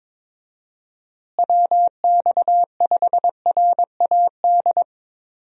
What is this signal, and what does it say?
Morse code, heard as a keyed tone, about 21 wpm, 700 Hz. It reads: WX5RAD
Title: USA Call Signs - 6 Characters - Repeated 3x - 22wpm